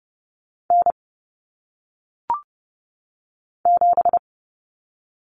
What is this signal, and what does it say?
Morse code, heard as a keyed tone, about 30 wpm, 700 Hz. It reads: N 7
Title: Common Cut Numbers - 1-Digit 30wpm